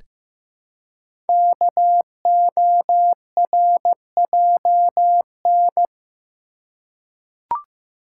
Morse code, heard as a keyed tone, about 15 wpm, 700 Hz. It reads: KORJN
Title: Code Groups - 5 Letters - 15wpm